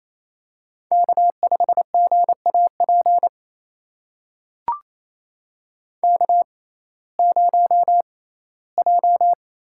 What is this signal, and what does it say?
Morse code, heard as a keyed tone, about 28 wpm, spaced as for 17 wpm, 700 Hz. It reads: K5GAP K0J
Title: USA Call Signs - 6x Character Spacing - 28wpm